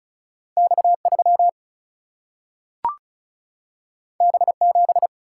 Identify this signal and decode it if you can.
Morse code, heard as a keyed tone, about 36 wpm, 700 Hz. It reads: X 3 B 7
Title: Single-Letter Single-Number 35wpm